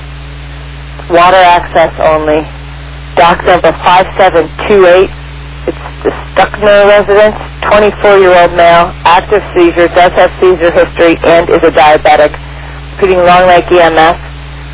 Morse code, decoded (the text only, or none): none